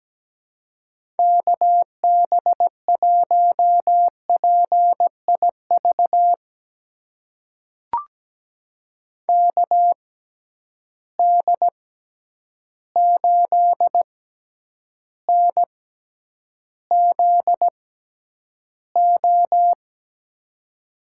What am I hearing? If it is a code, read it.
Morse code, heard as a keyed tone, about 17 wpm, spaced as for 9 wpm, 700 Hz. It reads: KB1PIV KD8NZO
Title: USA Call Signs - 6 Characters - 6x Character Spacing - 17wpm